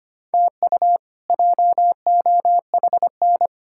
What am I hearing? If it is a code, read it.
Morse code, heard as a keyed tone, about 25 wpm, 700 Hz. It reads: TU JOHN
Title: K1USN SST QSO Practice - 25wpm